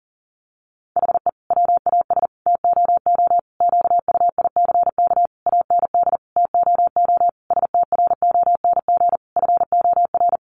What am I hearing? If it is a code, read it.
Morse code, heard as a keyed tone, about 40 wpm, 700 Hz. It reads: HE WAS TOO QUICK AND TOO STRONG FOR
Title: Sentences from Top 600 Words 40wpm